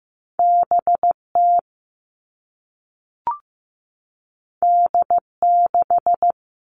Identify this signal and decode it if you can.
Morse code, heard as a keyed tone, about 15 wpm, 700 Hz. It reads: BT D6
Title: All Cut Numbers - 2-Digit 15wpm